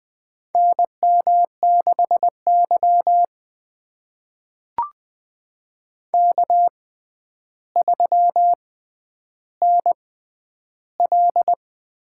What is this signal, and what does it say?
Morse code, heard as a keyed tone, about 20 wpm, spaced as for 10 wpm, 700 Hz. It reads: NM6Y K3NL
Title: USA Call Signs - 6x Character Spacing - 20wpm